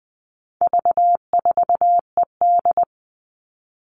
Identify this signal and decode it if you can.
Morse code, heard as a keyed tone, about 20 wpm, 700 Hz. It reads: V4ED
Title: All Cut Numbers - 4-Digit 20wpm